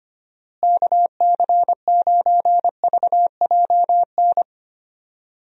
Morse code, heard as a keyed tone, about 25 wpm, 700 Hz. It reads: KC9VJN
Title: USA Call Signs - 6 Characters - Repeated 2x - 25wpm